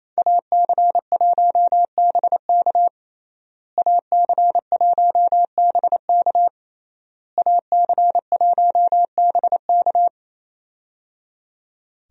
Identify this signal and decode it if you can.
Morse code, heard as a keyed tone, about 27 wpm, 700 Hz. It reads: AC1BK AC1BK AC1BK
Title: USA Call Signs - 5 Characters - Repeated 3x - 28wpm